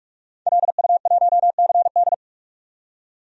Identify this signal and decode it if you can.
Morse code, heard as a keyed tone, about 46 wpm, 700 Hz. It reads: LU1CD
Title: DXCC-100 Call Signs - 5 Characters - 45wpm